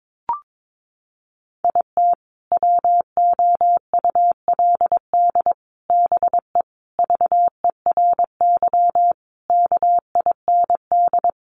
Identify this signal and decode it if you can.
Morse code, heard as a keyed tone, about 22 wpm, 700 Hz. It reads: IT WOULD BE VERY KIND
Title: Sentences from Top 200 Words 22wpm